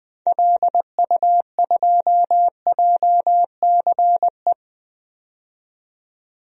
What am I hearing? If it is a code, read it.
Morse code, heard as a keyed tone, about 20 wpm, 700 Hz. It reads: LU2JCE
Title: DXCC-100 Call Signs - 6 Characters - 20wpm